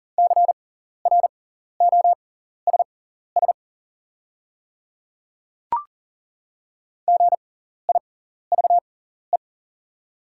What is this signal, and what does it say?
Morse code, heard as a keyed tone, about 40 wpm, spaced as for 14 wpm, 690 Hz. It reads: CROSS GIVE
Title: 6x Character Spacing - Top 300 Words - 40wpm